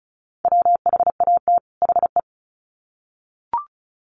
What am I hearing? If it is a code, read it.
Morse code, heard as a keyed tone, about 34 wpm, 700 Hz. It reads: WHAT HE
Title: Sets of 2 Words 35wpm